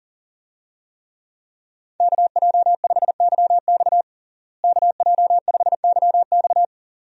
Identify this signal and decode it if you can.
Morse code, heard as a keyed tone, about 37 wpm, 700 Hz. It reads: KJ5YX KJ5YX
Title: USA Call Signs - 5 Characters - Repeated 2x - 40wpm